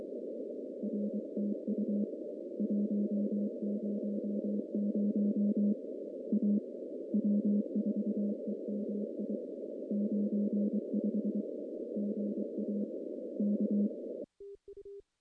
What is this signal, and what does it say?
Morse code, heard as a keyed tone, about 23 wpm, 210 Hz. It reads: R TU 100 A W4EMI 95 GA K
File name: contests / CW-SS-k5tr- -1400